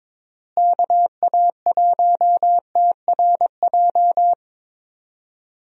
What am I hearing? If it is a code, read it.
Morse code, heard as a keyed tone, about 22 wpm, 700 Hz. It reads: KA1TRJ
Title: USA Call Signs - 6 Characters - Repeated 2x - 22wpm